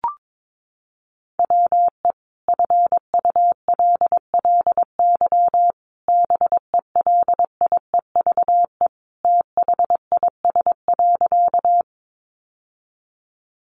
Morse code, heard as a keyed tone, about 22 wpm, 700 Hz. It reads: WE FULLY BELIEVE THIS.
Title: Sentences from Top 1300 Words 22wpm